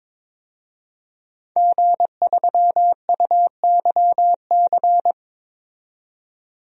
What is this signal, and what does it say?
Morse code, heard as a keyed tone, about 22 wpm, 700 Hz. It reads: G3UYC
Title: DXCC-100 Call Signs - 5 Characters - 22wpm